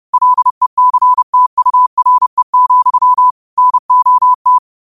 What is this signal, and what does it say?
Morse code, heard as a keyed tone, about 30 wpm, 1000 Hz. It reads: LECTURE, NOT
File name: qotdfiles / 30wpm